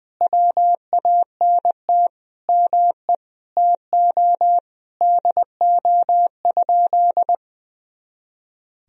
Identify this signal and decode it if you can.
Morse code, heard as a keyed tone, about 20 wpm, 700 Hz. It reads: WANT ME TO DO?
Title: Sentences from Top 200 Words and Abbreviations 20wpm